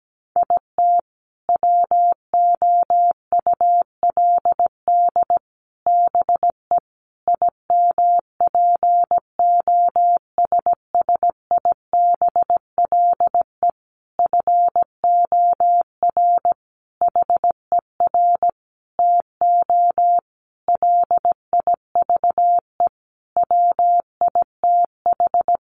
Morse code, heard as a keyed tone, about 17 wpm, 700 Hz. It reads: IT WOULD BE IMPOSSIBLE FOR HER TO LIVE WITH
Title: Sentences from Top 1100 Words 17wpm